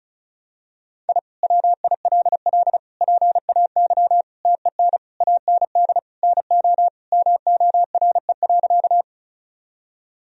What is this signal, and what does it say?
Morse code, heard as a keyed tone, about 34 wpm, 700 Hz. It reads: I WILL PAY TEN AND NO MORE.
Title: Sentences from Top 1000 Words - Review Part 2 - 35wpm